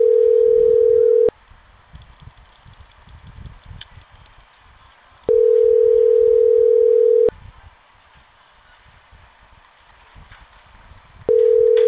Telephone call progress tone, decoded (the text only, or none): ringback tone